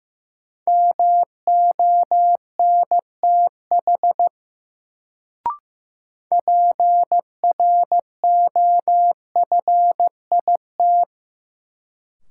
Morse code, heard as a keyed tone, about 15 wpm, 700 Hz. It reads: MONTH PROFIT
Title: Sentences from Top 800 Words - New Word Drill - 15wpm